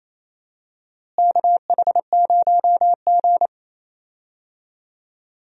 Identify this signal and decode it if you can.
Morse code, heard as a keyed tone, about 28 wpm, 700 Hz. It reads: KH0G
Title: USA Call Signs - 4 Characters - Repeated 3x - 28wpm